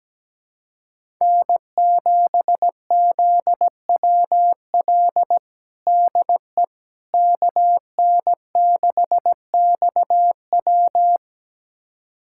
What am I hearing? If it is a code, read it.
Morse code, heard as a keyed tone, about 17 wpm, 700 Hz. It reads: N7ZWL DE KN6XW
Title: USA Callsign DE Callsign - 5 Characters - 17wpm